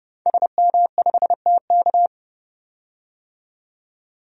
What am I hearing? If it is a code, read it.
Morse code, heard as a keyed tone, about 32 wpm, 700 Hz. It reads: SM5TK
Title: DXCC-100 Call Signs - 5 Characters - 30wpm